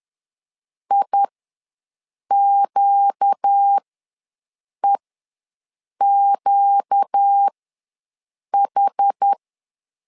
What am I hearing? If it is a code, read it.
Morse code, heard as a keyed tone, about 10 wpm, spaced as for 6 wpm, 800 Hz. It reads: IQEQH
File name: LEZIONI / 4 Q ws 10wpm 2306